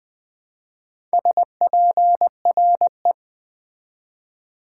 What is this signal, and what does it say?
Morse code, heard as a keyed tone, about 20 wpm, 700 Hz. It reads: SPRE
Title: Build A Word - 2 Letters At A Time - 6-Letter Words - 20wpm